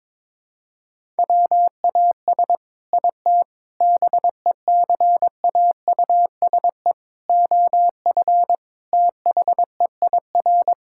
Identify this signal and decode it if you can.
Morse code, heard as a keyed tone, about 22 wpm, 700 Hz. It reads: WAS IT BECAUSE OF THEIR
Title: Sentences from Top 1000 Words - Review Part 6 - 22wpm